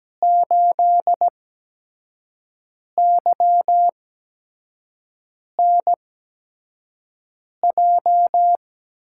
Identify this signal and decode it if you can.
Morse code, heard as a keyed tone, about 17 wpm, spaced as for 5 wpm, 700 Hz. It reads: 8YNJ
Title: USA Call Signs - 6 Characters - 8x Character Spacing - 17wpm